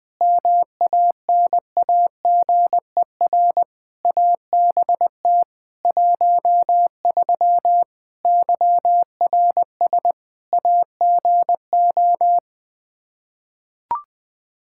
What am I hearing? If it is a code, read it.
Morse code, heard as a keyed tone, about 21 wpm, 700 Hz. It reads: MANAGER ABT 13 YRS AGO
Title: K1USN SST QSO Practice 20wpm